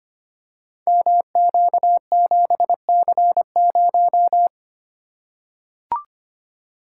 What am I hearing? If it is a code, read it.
Morse code, heard as a keyed tone, about 25 wpm, 700 Hz. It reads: MQ7C0
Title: Code Groups - 5 Letter:Numbers - 25wpm